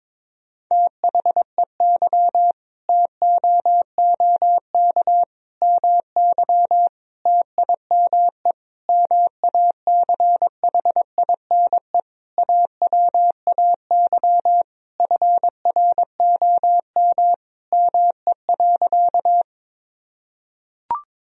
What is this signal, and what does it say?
Morse code, heard as a keyed tone, about 22 wpm, 700 Hz. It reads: THEY TOOK MY TIME MACHINE AWAY FROM ME.
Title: Sentences from Top 1000 Words - Review Part 2 - 22wpm